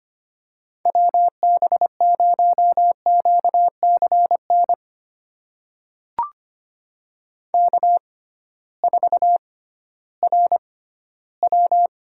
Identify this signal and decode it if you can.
Morse code, heard as a keyed tone, about 25 wpm, spaced as for 15 wpm, 700 Hz. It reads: WB0QCN K4RW
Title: USA Call Signs - 6x Character Spacing - 25wpm